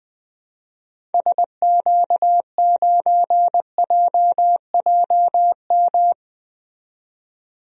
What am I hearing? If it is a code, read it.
Morse code, heard as a keyed tone, about 20 wpm, 700 Hz. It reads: SQ9JJM
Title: DXCC-100 Call Signs - 6 Characters - 20wpm